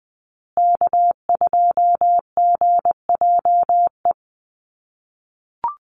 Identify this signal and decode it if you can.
Morse code, heard as a keyed tone, about 20 wpm, 700 Hz. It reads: K2GJE
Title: USA Call Signs Repeated 2x 20wpm